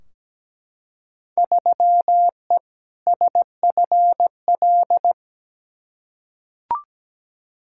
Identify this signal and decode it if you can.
Morse code, heard as a keyed tone, about 17 wpm, 700 Hz. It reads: 3E SFL
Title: Field Day Exchanges Encore - 17wpm